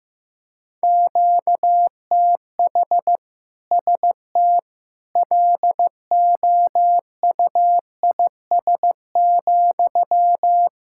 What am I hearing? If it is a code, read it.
Morse code, heard as a keyed tone, about 15 wpm, 700 Hz. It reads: QTH ST LOUIS,
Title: Practice QSO - Exchange 1 - Beginning - 15wpm